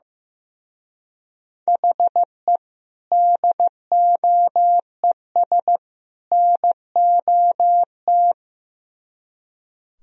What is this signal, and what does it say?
Morse code, heard as a keyed tone, about 17 wpm, 700 Hz. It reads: HE DOES NOT
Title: Speed-Racing - Sets of 3 Words 15wpm